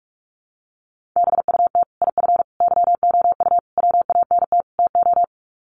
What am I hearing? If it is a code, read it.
Morse code, heard as a keyed tone, about 44 wpm, 700 Hz. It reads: BUT IF YOU WANT TO
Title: Sets of 5 Words Encore2 45wpm